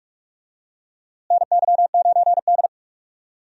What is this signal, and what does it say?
Morse code, heard as a keyed tone, about 43 wpm, 700 Hz. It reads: NY9D
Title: USA Call Signs - 4 Characters - Repeated 2x - 45wpm